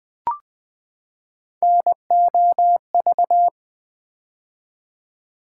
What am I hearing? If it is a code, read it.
Morse code, heard as a keyed tone, about 20 wpm, 700 Hz. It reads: NOV
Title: Build A Word - 3 Letters At A Time - 8-Letter Words - 20wpm